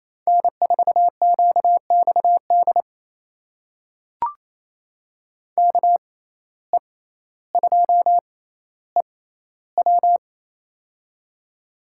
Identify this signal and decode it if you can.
Morse code, heard as a keyed tone, about 28 wpm, spaced as for 14 wpm, 700 Hz. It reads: N4QXD KE2EW
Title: USA Call Signs - 5 Characters - 6x Character Spacing - 28wpm